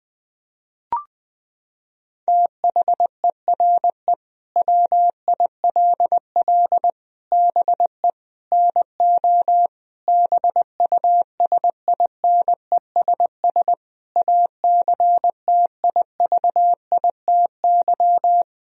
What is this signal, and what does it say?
Morse code, heard as a keyed tone, about 20 wpm, 700 Hz. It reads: THERE WILL BE NO BUSINESS ACTIVITY
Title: Sentences from Top 1000 Words - Review Part 4 - 20wpm